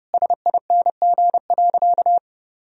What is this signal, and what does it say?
Morse code, heard as a keyed tone, about 30 wpm, 700 Hz. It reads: SING.
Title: Sentences from Top 1000 Words - Review Part 2 - 30wpm